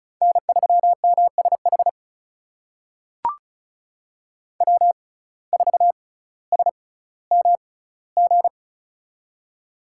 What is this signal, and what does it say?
Morse code, heard as a keyed tone, about 36 wpm, spaced as for 18 wpm, 700 Hz. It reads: N3MSH W4SMG
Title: USA Call Signs - 5 Characters - 6x Character Spacing - 35wpm